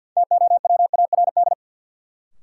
Morse code, heard as a keyed tone, about 51 wpm, 700 Hz. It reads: TOWARD
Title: Sentences from Top 1000 Words - Review Part 6 - New Word Drill - 50wpm